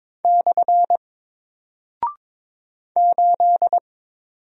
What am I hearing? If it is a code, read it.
Morse code, heard as a keyed tone, about 22 wpm, 700 Hz. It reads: / 8
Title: Single Character (Letter:Number:Punctuation:Prosign) Alt 22wpm